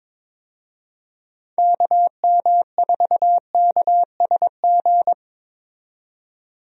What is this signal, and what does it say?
Morse code, heard as a keyed tone, about 22 wpm, 700 Hz. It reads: KM4KSG
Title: USA Call Signs - 6 Characters - 22wpm